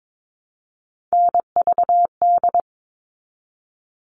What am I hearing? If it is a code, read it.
Morse code, heard as a keyed tone, about 22 wpm, 700 Hz. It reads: NVD
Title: All Cut Numbers - 3-Digit 22wpm